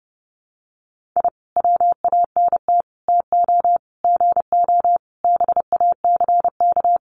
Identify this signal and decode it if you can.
Morse code, heard as a keyed tone, about 30 wpm, 700 Hz. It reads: I WANT TO GO BACK
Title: Sets of 5 Words Encore 30wpm